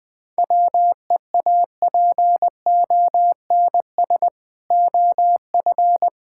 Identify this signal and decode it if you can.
Morse code, heard as a keyed tone, about 20 wpm, 690 Hz. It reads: WEAPONS OF
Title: Build A Phrase - 4-Word Phrases - 20wpm